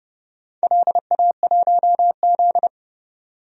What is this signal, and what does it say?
Morse code, heard as a keyed tone, about 30 wpm, 700 Hz. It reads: LA1Z
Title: DXCC-100 Call Signs - 4 Characters - 30wpm